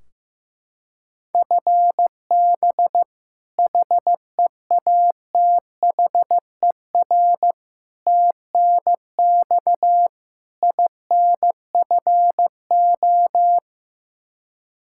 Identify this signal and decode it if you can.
Morse code, heard as a keyed tone, about 15 wpm, 700 Hz. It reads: FB HEATHER TNX INFO
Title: Practice QSO - Exchange 3 - More Info - 15wpm